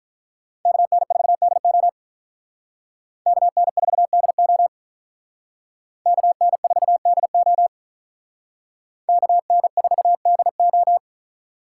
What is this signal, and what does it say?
Morse code, heard as a keyed tone, about 46 wpm, 700 Hz. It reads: KN4DO KN4DO KN4DO KN4DO